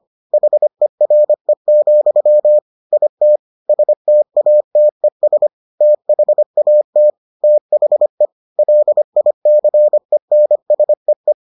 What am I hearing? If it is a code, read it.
Morse code, heard as a keyed tone, about 25 wpm, 600 Hz. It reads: HERE, IT STATES THAT THE LICENSEE